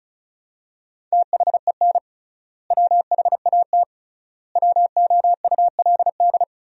Morse code, heard as a keyed tone, about 34 wpm, 700 Hz. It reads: THEN WHAT WOULD
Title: 3x Word-Spacing - Sentences from Top 400 Words - 35wpm